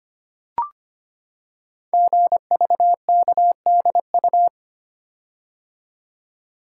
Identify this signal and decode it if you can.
Morse code, heard as a keyed tone, about 25 wpm, 690 Hz. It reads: GVKDU
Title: Code Groups - 5 Letter:Numbers - 25wpm